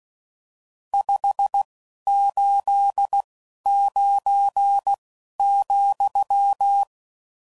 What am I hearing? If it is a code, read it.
Morse code, heard as a keyed tone, about 16 wpm, spaced as for 11 wpm, 800 Hz. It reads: 589,
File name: CW Code Groups @10WPM / RSTs / Various RST No's